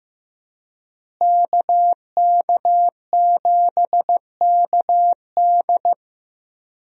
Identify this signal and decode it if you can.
Morse code, heard as a keyed tone, about 15 wpm, 700 Hz. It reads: KK7KD
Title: USA Call Signs - 5 Characters - Repeated 3x - 15wpm